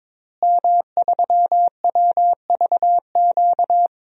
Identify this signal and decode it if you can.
Morse code, heard as a keyed tone, about 22 wpm, 700 Hz. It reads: M3WVQ
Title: DXCC-100 Call Signs - 5 Characters - Repeated 2x - 22wpm